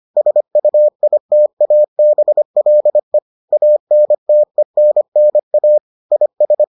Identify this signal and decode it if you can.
Morse code, heard as a keyed tone, about 25 wpm, 600 Hz. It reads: SUITABLE ANTENNA IS